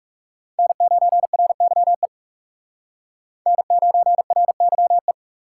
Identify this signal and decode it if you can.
Morse code, heard as a keyed tone, about 36 wpm, 700 Hz. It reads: N9RYE N9RYE
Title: Speed Racing - USA 1x3 Call Signs 30wpm